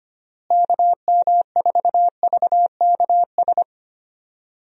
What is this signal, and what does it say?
Morse code, heard as a keyed tone, about 25 wpm, 700 Hz. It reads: KM4VKS
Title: USA Call Signs - 6 Characters - Repeated 2x - 25wpm